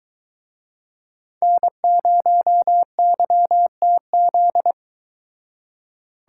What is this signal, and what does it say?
Morse code, heard as a keyed tone, about 23 wpm, 700 Hz. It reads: N0YTZ